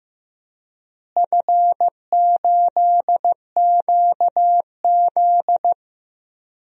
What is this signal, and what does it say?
Morse code, heard as a keyed tone, about 15 wpm, 700 Hz. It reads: F8QZ
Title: DXCC-100 Call Signs - 4 Characters - Repeated 2x - 15wpm